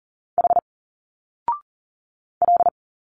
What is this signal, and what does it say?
Morse code, heard as a keyed tone, about 40 wpm, 700 Hz. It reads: H L
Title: Single Letters 40wpm